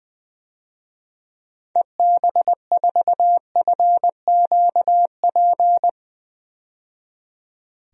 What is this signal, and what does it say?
Morse code, heard as a keyed tone, about 20 wpm, 700 Hz. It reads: EB4FQP
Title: DXCC-100 Call Signs - 6 Characters - 20wpm